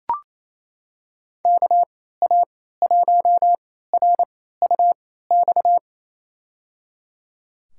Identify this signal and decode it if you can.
Morse code, heard as a keyed tone, about 29 wpm, spaced as for 16 wpm, 700 Hz. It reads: KA1RUX
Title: USA Call Signs - 6 Characters - 3x Character Spacing - 28wpm